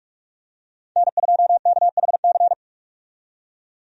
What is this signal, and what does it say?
Morse code, heard as a keyed tone, about 45 wpm, 700 Hz. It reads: N2KHC
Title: USA Call Signs - 5 Characters - 45wpm